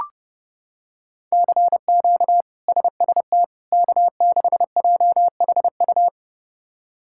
Morse code, heard as a keyed tone, about 30 wpm, 700 Hz. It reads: CQ SST K6JHU
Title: K1USN SST QSO Practice - 30wpm